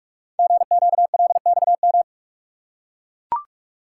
Morse code, heard as a keyed tone, about 44 wpm, 690 Hz. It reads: GQLXM
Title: Code Groups - 5 Letters - 45wpm